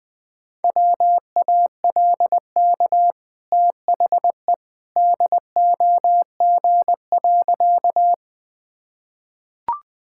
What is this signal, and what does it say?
Morse code, heard as a keyed tone, about 20 wpm, 700 Hz. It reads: WALK THE DOG.
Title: Easy Sentences with Hints - 20wpm